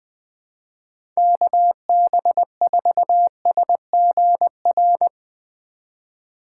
Morse code, heard as a keyed tone, about 20 wpm, 700 Hz. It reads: KB4SGR
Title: USA Call Signs - 6 Characters - 3x Character Spacing - 20wpm